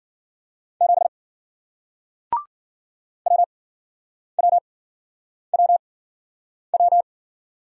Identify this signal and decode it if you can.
Morse code, heard as a keyed tone, about 55 wpm, spaced as for 9 wpm, 700 Hz. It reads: 7 WWWW